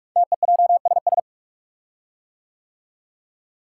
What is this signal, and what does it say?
Morse code, heard as a keyed tone, about 44 wpm, 690 Hz. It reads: TEJSS
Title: Code Groups - 5 Letters - 45wpm